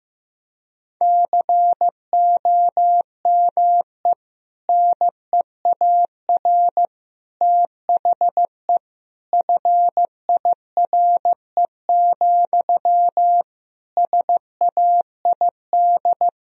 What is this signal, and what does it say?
Morse code, heard as a keyed tone, about 16 wpm, 700 Hz. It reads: COME NEAR THE FIRE, SAID
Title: Sentences from Top 1000 Words - Review Part 1 - 15wpm